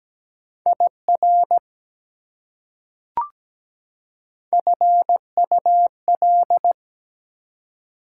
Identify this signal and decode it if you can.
Morse code, heard as a keyed tone, about 17 wpm, 700 Hz. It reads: IR FUL
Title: Prefixes and Suffixes 17wpm